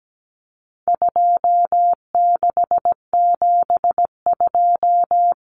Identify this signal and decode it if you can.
Morse code, heard as a keyed tone, about 17 wpm, 700 Hz. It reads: 2672
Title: Numbers - 4-Digit 17wpm